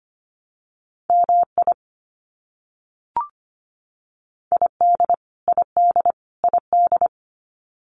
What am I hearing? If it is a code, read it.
Morse code, heard as a keyed tone, about 25 wpm, 700 Hz. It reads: MI ID ID ID
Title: US States Repeated 3x 25wpm